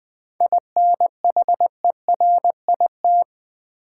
Morse code, heard as a keyed tone, about 21 wpm, 700 Hz. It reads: INHERIT
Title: 7 Letter Words - First 3 Letters Given - 20wpm